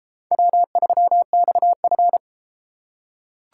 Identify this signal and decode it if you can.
Morse code, heard as a keyed tone, about 31 wpm, 700 Hz. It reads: W3XF